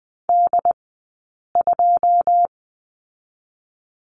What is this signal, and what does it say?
Morse code, heard as a keyed tone, about 20 wpm, 700 Hz. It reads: D 2
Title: 086 Lesson 15d - Number 5 - Two Characters 20wpm